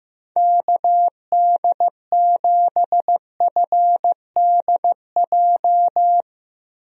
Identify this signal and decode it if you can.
Morse code, heard as a keyed tone, about 15 wpm, 700 Hz. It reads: KD7FDJ
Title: USA Call Signs - 6 Characters - Repeated 3x - 15wpm